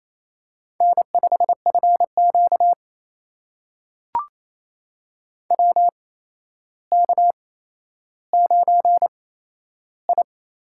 Morse code, heard as a keyed tone, about 27 wpm, spaced as for 12 wpm, 700 Hz. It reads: N5FQ WK9I
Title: USA Call Signs - 4 Characters - 8x Character Spacing - 28wpm